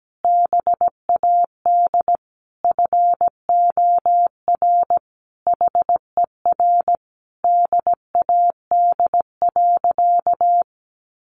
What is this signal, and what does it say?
Morse code, heard as a keyed tone, about 17 wpm, 700 Hz. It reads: BAD FOR HER DAD.
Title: Additional Easy Phrases 17wpm